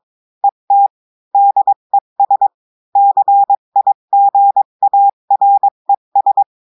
Morse code, heard as a keyed tone, about 22 wpm, 800 Hz. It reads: ET DES CIGARES
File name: cours-du-mercredi / audios / cours-853-du-2-mai-2024-etxte chiffre et texte clair-a-20-wpm